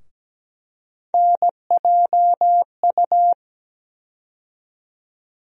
Morse code, heard as a keyed tone, about 17 wpm, 700 Hz. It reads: NJU
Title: USA Call Sign Postfixes - 3 Letters 17wpm